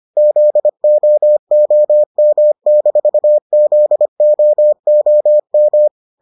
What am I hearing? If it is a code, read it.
Morse code, heard as a keyed tone, about 25 wpm, 600 Hz. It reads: ZOOM-ZOOM